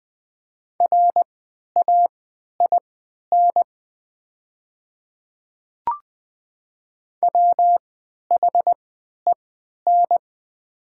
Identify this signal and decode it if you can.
Morse code, heard as a keyed tone, about 20 wpm, spaced as for 11 wpm, 700 Hz. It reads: RAIN WHEN
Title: 3x Character Spacing - Top 300 Words - 20wpm